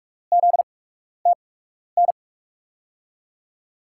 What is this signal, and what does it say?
Morse code, heard as a keyed tone, about 46 wpm, spaced as for 12 wpm, 700 Hz. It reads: ZTN
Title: USA Call Signs - 6 Characters - 8x Character Spacing - 45wpm